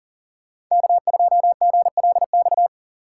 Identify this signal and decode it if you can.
Morse code, heard as a keyed tone, about 40 wpm, 700 Hz. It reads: K2GLX
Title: USA Call Signs - 2x Character Spacing - 40wpm